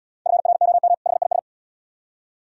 Morse code, heard as a keyed tone, about 76 wpm, 700 Hz. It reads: FROM HIS